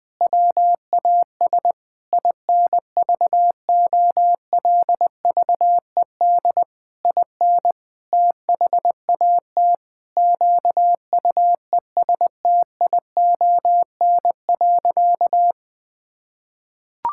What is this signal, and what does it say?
Morse code, heard as a keyed tone, about 20 wpm, 700 Hz. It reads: WAS INVOLVED IN THAT QUESTION.
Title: Sentences from Top 1000 Words - Review Part 5 - 20wpm